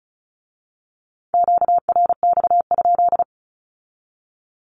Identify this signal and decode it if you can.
Morse code, heard as a keyed tone, about 34 wpm, 700 Hz. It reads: QRX?
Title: Q-Codes with Spoken Meaning - 35wpm